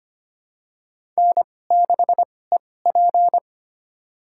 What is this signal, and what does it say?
Morse code, heard as a keyed tone, about 25 wpm, spaced as for 18 wpm, 700 Hz. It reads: N6EP
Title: USA Call Signs - 2x Character Spacing - 25wpm